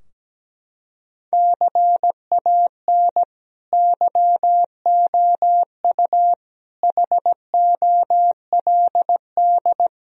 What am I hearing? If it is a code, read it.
Morse code, heard as a keyed tone, about 17 wpm, 700 Hz. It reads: CAN YOU HOLD
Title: Sentences from Top 1000 Words - Review Part 3 - 17wpm